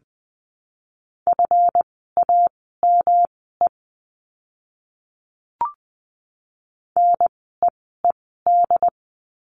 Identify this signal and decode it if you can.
Morse code, heard as a keyed tone, about 20 wpm, spaced as for 15 wpm, 700 Hz. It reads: FAME NEED
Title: 139 Lesson 23f - Letter f - Words 20wpm